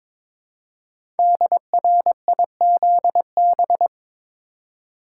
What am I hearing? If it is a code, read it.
Morse code, heard as a keyed tone, about 22 wpm, 700 Hz. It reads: DRIZB
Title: Code Groups - 5 Letters - 22wpm